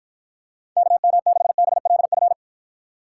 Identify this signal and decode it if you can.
Morse code, heard as a keyed tone, about 53 wpm, 700 Hz. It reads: KM6BLF